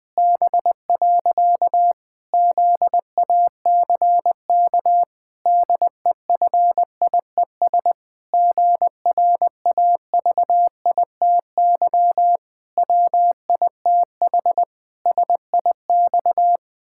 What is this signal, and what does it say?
Morse code, heard as a keyed tone, about 21 wpm, 700 Hz. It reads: B. ZACK DEFIES GRAVITY WITH SIX
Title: Pangram Challenge v2 - 20wpm